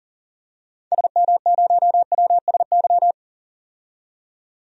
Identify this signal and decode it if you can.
Morse code, heard as a keyed tone, about 40 wpm, 700 Hz. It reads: SM0WSY
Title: DXCC-100 Call Signs - 6 Characters - 40wpm